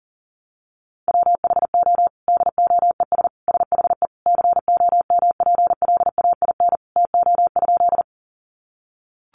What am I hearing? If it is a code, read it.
Morse code, heard as a keyed tone, about 40 wpm, 700 Hz. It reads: WHO DOES SHE COMPLAIN TO?
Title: Sentences from Top 1300 Words 40wpm